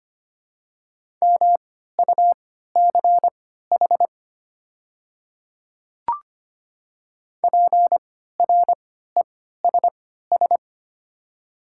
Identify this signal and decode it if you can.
Morse code, heard as a keyed tone, about 25 wpm, spaced as for 14 wpm, 700 Hz. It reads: MUCH PRESS
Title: 3x Character Spacing - Top 300 Words - 25wpm